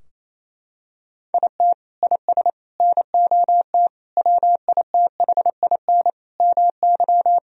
Morse code, heard as a keyed tone, about 28 wpm, 700 Hz. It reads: IT IS NOT WITHIN MY
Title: Sentences from Top 1000 Words - Review Part 4 - 28wpm